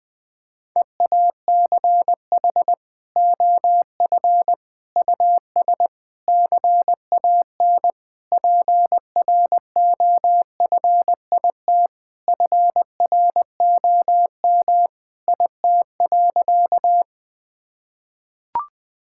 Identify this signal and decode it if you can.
Morse code, heard as a keyed tone, about 20 wpm, 700 Hz. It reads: EACH OF US CAN PROFIT FROM IT.
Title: Sentences from Top 1000 Words - Review Part 5 - 20wpm